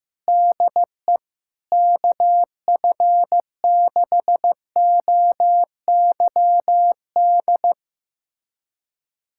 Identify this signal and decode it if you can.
Morse code, heard as a keyed tone, about 15 wpm, 700 Hz. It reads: DE KF6OYD
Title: USA Callsign DE Callsign - 6 Characters - 15wpm